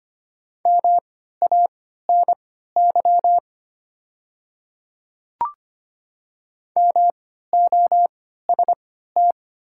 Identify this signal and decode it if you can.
Morse code, heard as a keyed tone, about 25 wpm, spaced as for 14 wpm, 700 Hz. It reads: MANY MOST
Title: 3x Character Spacing - Top 100 Words - 25wpm